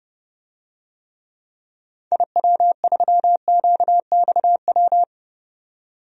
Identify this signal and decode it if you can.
Morse code, heard as a keyed tone, about 30 wpm, 700 Hz. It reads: IW3QXW
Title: DXCC-100 Call Signs - 6 Characters - 30wpm